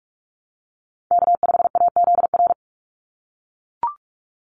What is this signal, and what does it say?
Morse code, heard as a keyed tone, about 44 wpm, 700 Hz. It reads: K5AZR
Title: Call Signs - 45wpm